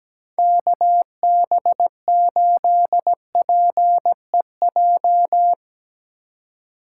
Morse code, heard as a keyed tone, about 19 wpm, 700 Hz. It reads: KB8PEJ
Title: Speed Racing - USA 2x3 Call Signs 17wpm